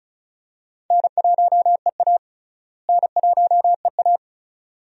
Morse code, heard as a keyed tone, about 36 wpm, 700 Hz. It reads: N1EA N1EA
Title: USA Call Signs - 4 Characters - Repeated 2x - 35wpm